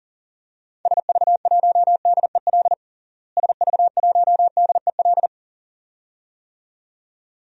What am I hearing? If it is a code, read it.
Morse code, heard as a keyed tone, about 39 wpm, 700 Hz. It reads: SV1DEL SV1DEL
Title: DXCC-100 Call Signs - 6 Characters - Repeated 2x - 40wpm